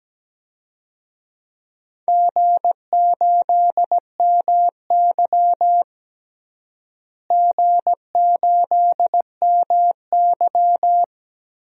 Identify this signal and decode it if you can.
Morse code, heard as a keyed tone, about 17 wpm, 700 Hz. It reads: G8MY G8MY
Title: DXCC-100 Call Signs - 4 Characters - Repeated 3x - 17wpm